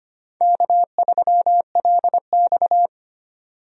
Morse code, heard as a keyed tone, about 25 wpm, 700 Hz. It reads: K3LX
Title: USA Call Signs - 4 Characters - 25wpm